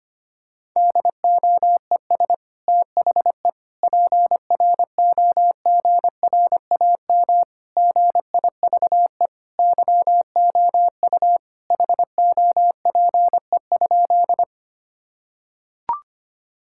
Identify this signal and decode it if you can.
Morse code, heard as a keyed tone, about 25 wpm, 700 Hz. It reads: DOES THE PROGRAM GIVE YOU HOPE?
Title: Sentences from Top 1000 Words - Review Part 5 - 25wpm